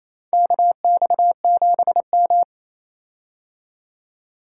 Morse code, heard as a keyed tone, about 28 wpm, 700 Hz. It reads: KX7M
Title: K1USN SST QSO Practice - 28wpm